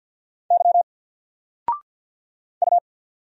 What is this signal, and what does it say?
Morse code, heard as a keyed tone, about 48 wpm, 700 Hz. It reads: Y U
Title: Single Character (Letter:Number:Punctuation:Prosign) Alt 50wpm